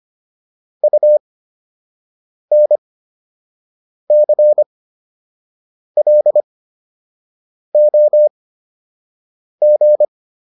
Morse code, heard as a keyed tone, about 25 wpm, spaced as for 6 wpm, 600 Hz. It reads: UNCLOG